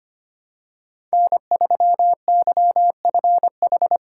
Morse code, heard as a keyed tone, about 25 wpm, 700 Hz. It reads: N3YFH
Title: USA Call Signs - 5 Characters - 3x Character Spacing - 25wpm